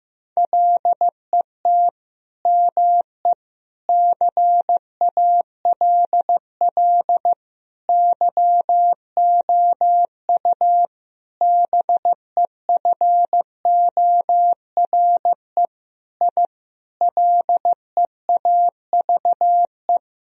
Morse code, heard as a keyed tone, about 15 wpm, 700 Hz. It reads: LET ME CALL YOU BEFORE I LEAVE
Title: Sentences from Top 1000 Words - Review Part 2 - 15wpm